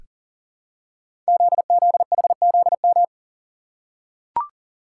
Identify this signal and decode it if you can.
Morse code, heard as a keyed tone, about 39 wpm, 700 Hz. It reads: ZZHZM
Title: Code Groups - 5 Letters - 40wpm